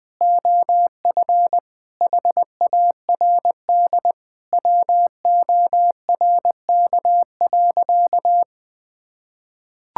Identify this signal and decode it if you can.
Morse code, heard as a keyed tone, about 20 wpm, 700 Hz. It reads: OF HARD WORK.
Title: Sentences with 3-Word Phrases - 20wpm